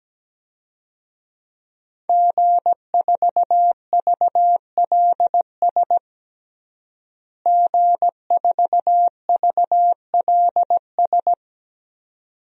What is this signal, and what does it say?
Morse code, heard as a keyed tone, about 17 wpm, 700 Hz. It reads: G4VLS G4VLS
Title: DXCC-100 Call Signs - 5 Characters - Repeated 2x - 17wpm